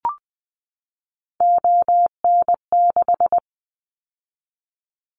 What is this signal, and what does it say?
Morse code, heard as a keyed tone, about 20 wpm, 700 Hz. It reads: ON6
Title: Callsign Prefixes for Top 100 DXCC - 20wpm